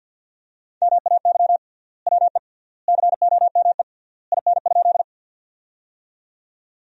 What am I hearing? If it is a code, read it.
Morse code, heard as a keyed tone, about 50 wpm, 700 Hz. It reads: MAY WE COME IN?
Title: 3x Word-Spacing - Sentences from Top 100 Words - 50wpm